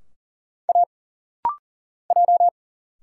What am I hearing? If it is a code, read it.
Morse code, heard as a keyed tone, about 40 wpm, 700 Hz. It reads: A J
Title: Single Letters - Rapid-Fire 40wpm